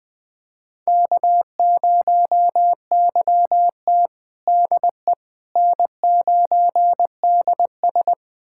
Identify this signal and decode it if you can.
Morse code, heard as a keyed tone, about 20 wpm, 700 Hz. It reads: K0YT DE N9DS
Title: USA Callsign DE Callsign - 4 Characters - 20wpm